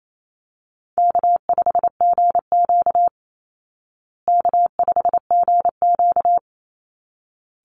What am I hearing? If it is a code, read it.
Morse code, heard as a keyed tone, about 28 wpm, 700 Hz. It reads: K5GQ K5GQ
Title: Easy Call Signs Repeated 2x - 28wpm